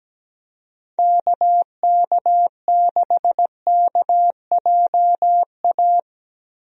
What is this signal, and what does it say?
Morse code, heard as a keyed tone, about 17 wpm, 700 Hz. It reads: KK6KJA
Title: USA Call Signs - 6 Characters - Repeated 3x - 17wpm